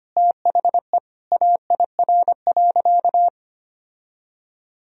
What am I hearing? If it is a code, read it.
Morse code, heard as a keyed tone, about 25 wpm, 700 Hz. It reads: THE AIR.
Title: Sentences from Top 1000 Words - Review Part 1 - 25wpm